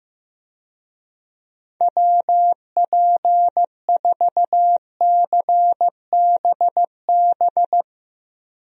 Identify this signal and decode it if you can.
Morse code, heard as a keyed tone, about 15 wpm, 700 Hz. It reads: WP4CBB
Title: USA Call Signs - 6 Characters - Repeated 3x - 15wpm